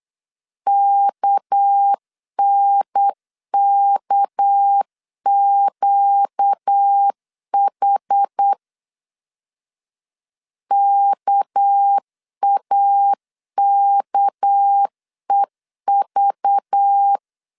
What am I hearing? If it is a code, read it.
Morse code, heard as a keyed tone, about 8 wpm, 800 Hz. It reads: KNKQH KAKEV
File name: LEZIONI / 6 PKX0 / 6 PK 8wpm 3003